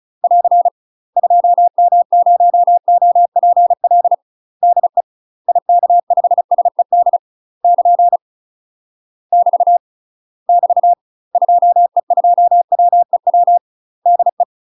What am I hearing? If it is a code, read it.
Morse code, heard as a keyed tone, about 34 wpm, 700 Hz. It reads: <AR> 2M0OPL DE IK5SED <KN> <BT> <BT> 2E2WEW DE
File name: UK MASTER QSOs TRAINING / 35WPM / UK MASTER QSOs 0008